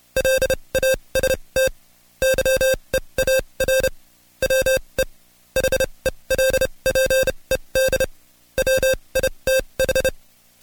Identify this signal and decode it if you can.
Morse code, heard as a keyed tone, about 29 wpm, spaced as for 23 wpm, 530 Hz. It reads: LAST YEAR WE HELPED WITH
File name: Code Quick / AudioDownld2 / Seg138-Stepper11 23WPM